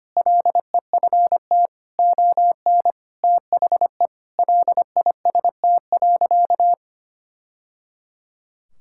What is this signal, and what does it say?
Morse code, heard as a keyed tone, about 25 wpm, 700 Hz. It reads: LEFT ON THE LIST.
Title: Sentences from Top 1000 Words - Review Part 6 - 25wpm